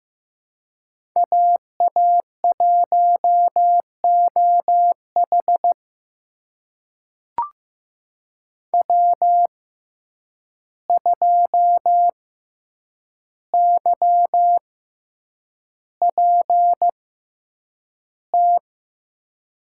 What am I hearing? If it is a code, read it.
Morse code, heard as a keyed tone, about 15 wpm, spaced as for 8 wpm, 700 Hz. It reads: AA1OH W2YPT
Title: USA Call Signs - 5 Characters - 6x Character Spacing - 15wpm